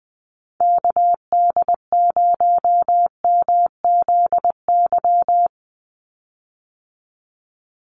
Morse code, heard as a keyed tone, about 20 wpm, 700 Hz. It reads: KD0MZY
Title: USA Call Signs 20wpm